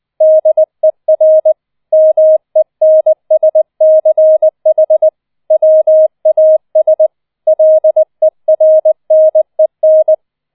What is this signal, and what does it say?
Morse code, heard as a keyed tone, about 20 wpm, 630 Hz. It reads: DER MENSCH WAS LERNEN
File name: 004 Vierter Streich 20wpm